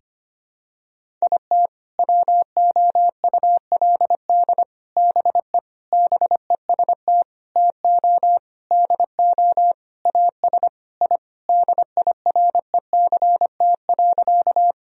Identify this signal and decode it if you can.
Morse code, heard as a keyed tone, about 25 wpm, 700 Hz. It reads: IT WOULD BE BEST TO DO AS I DIRECT.
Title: Sentences from Top 1000 Words - Review Part 2 - 25wpm